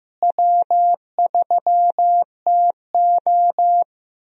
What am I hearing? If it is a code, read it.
Morse code, heard as a keyed tone, about 15 wpm, 700 Hz. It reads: W3TO
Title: USA Call Signs - 4 Characters - Repeated 3x - 15wpm